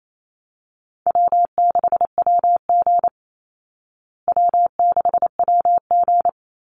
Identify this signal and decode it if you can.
Morse code, heard as a keyed tone, about 28 wpm, 700 Hz. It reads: W6WG W6WG
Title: Call Signs Repeated 3x - 28wpm